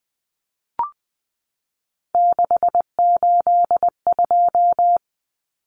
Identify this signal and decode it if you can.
Morse code, heard as a keyed tone, about 20 wpm, 700 Hz. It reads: 682
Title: Common Cut Numbers - 3-Digit 20wpm